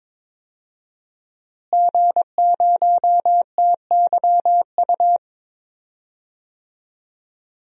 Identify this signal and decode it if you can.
Morse code, heard as a keyed tone, about 22 wpm, 700 Hz. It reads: G0TYU
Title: DXCC-100 Call Signs - 5 Characters - Repeated 2x - 22wpm